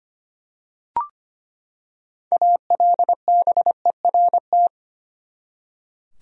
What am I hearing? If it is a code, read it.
Morse code, heard as a keyed tone, about 25 wpm, 700 Hz. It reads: ALBERT
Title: USA Common Names - 25wpm